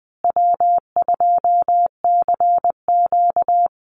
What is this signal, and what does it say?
Morse code, heard as a keyed tone, about 20 wpm, 700 Hz. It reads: W2CQ
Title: Easy Call Signs Repeated 2x - 20wpm